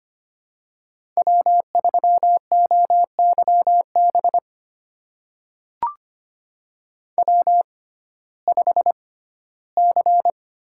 Morse code, heard as a keyed tone, about 25 wpm, spaced as for 15 wpm, 700 Hz. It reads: W3OYB W5C
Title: USA Call Signs - 5 Characters - 6x Character Spacing - 25wpm